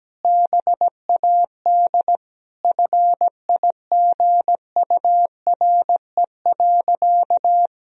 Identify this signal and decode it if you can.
Morse code, heard as a keyed tone, about 17 wpm, 700 Hz. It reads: BAD FIGURE.
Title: Sentences from Top 1000 Words - Review Part 2 - 17wpm